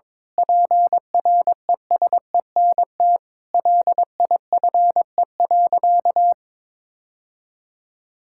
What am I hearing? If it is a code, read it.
Morse code, heard as a keyed tone, about 23 wpm, 700 Hz. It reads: PRESENT LIFE.
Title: Sentences from Top 1000 Words - Review Part 6 - 22wpm